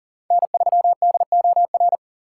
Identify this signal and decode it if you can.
Morse code, heard as a keyed tone, about 40 wpm, 700 Hz. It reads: N3DOR
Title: USA Call Signs - 5 Characters - 4x Character Spacing - 40wpm